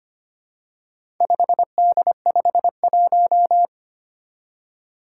Morse code, heard as a keyed tone, about 25 wpm, 700 Hz. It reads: 5D51
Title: Four Challenging Letter-Number Combinations 25wpm